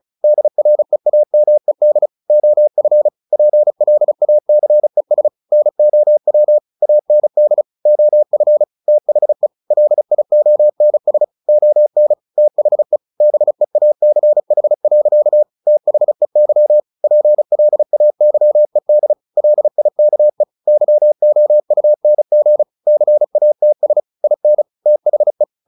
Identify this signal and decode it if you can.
Morse code, heard as a keyed tone, about 34 wpm, 600 Hz. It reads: DREAMED OF PLACES NOW AND OF THE LIONS ON THE BEACH. THEY PLAYED LIKE YOUNG CATS IN THE
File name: the old man and the sea / the old man and the sea 1 35wpm